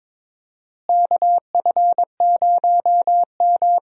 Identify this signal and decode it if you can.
Morse code, heard as a keyed tone, about 22 wpm, 700 Hz. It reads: KF0M
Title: USA Call Signs - 4 Characters - Repeated 2x - 22wpm